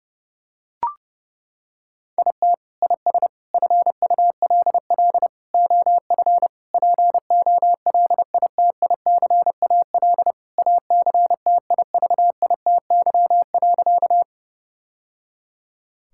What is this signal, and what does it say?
Morse code, heard as a keyed tone, about 32 wpm, 700 Hz. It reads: IT IS FULL OF POLITICAL ACTIVITY.
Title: Sentences from Top 1000 Words - Review Part 5 - 30wpm